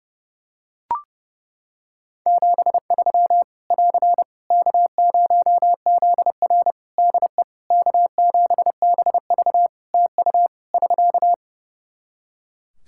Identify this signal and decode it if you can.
Morse code, heard as a keyed tone, about 30 wpm, 700 Hz. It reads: 73 <AR> K0ZR DE K7BV TU <SK>
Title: K1USN SST QSO Practice 30wpm